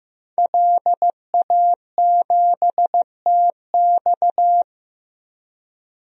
Morse code, heard as a keyed tone, about 15 wpm, 700 Hz. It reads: LA7TX
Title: DXCC-100 Call Signs - 5 Characters - 15wpm